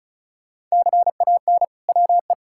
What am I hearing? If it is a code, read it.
Morse code, heard as a keyed tone, about 36 wpm, 700 Hz. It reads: CAN WE
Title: Sentences from Top 1000 Words - Review Part 5 - 35wpm